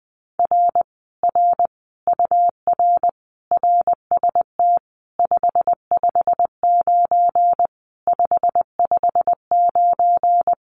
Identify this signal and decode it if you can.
Morse code, heard as a keyed tone, about 20 wpm, 700 Hz. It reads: R R UR RST 559 559
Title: Common QSO Phrases Encore 20wpm